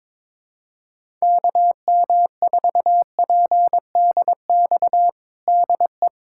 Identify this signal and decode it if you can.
Morse code, heard as a keyed tone, about 22 wpm, 700 Hz. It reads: KM4PDX DE
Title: USA Callsign DE Callsign - 6 Characters - 22wpm